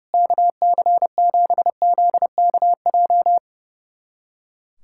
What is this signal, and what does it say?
Morse code, heard as a keyed tone, about 30 wpm, 700 Hz. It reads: KC7ZKJ
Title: USA Callsign DE Callsign - 6 Characters - 30wpm